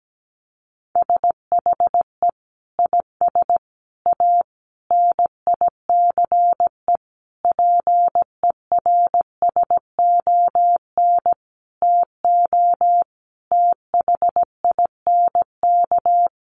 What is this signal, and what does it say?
Morse code, heard as a keyed tone, about 17 wpm, 700 Hz. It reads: SHE IS A NICE PERSON TO THINK
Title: Sentences from Top 1100 Words 17wpm